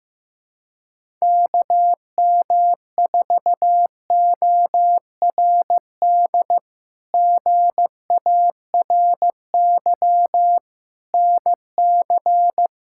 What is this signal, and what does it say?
Morse code, heard as a keyed tone, about 16 wpm, 700 Hz. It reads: KM4ORD GARY NC
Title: K1USN SST QSO Practice - 15wpm